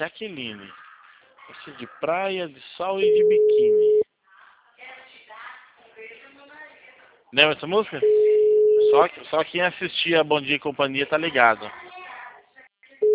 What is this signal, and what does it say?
Ringback tone (Europe, 425 Hz), with speech.
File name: out- -2020- - - .31866